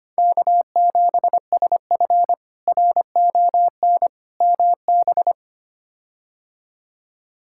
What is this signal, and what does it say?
Morse code, heard as a keyed tone, about 25 wpm, 700 Hz. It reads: K7SF RON MB
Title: K1USN SST QSO Practice - 25wpm